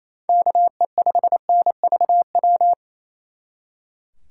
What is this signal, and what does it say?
Morse code, heard as a keyed tone, about 37 wpm, 700 Hz. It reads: KE5NVW